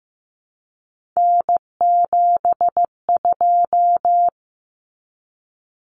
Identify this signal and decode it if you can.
Morse code, heard as a keyed tone, about 15 wpm, 700 Hz. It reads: N72
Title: Common Cut Numbers - 3-Digit 15wpm